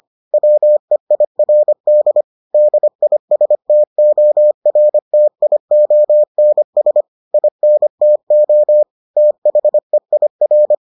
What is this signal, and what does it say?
Morse code, heard as a keyed tone, about 25 wpm, 600 Hz. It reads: WEIRD DISTORTIONS INTO THEIR